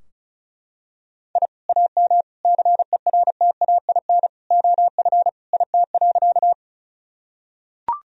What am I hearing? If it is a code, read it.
Morse code, heard as a keyed tone, about 34 wpm, 700 Hz. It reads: I AM CERTAIN OF IT.
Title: Sentences from Top 1000 Words - Review Part 2 - 35wpm